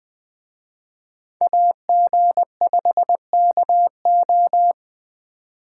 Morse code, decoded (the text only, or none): AG5KO